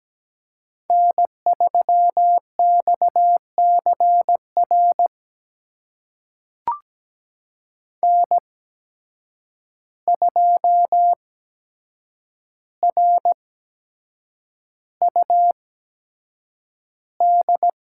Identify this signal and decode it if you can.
Morse code, heard as a keyed tone, about 17 wpm, spaced as for 7 wpm, 700 Hz. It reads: N3XCR N2RUD
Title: USA Call Signs - 5 Characters - 8x Character Spacing - 17wpm